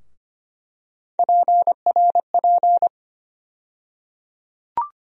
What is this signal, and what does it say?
Morse code, heard as a keyed tone, about 25 wpm, 700 Hz. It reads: PRP
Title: USA Call Sign Postfixes - 3 Letters 25wpm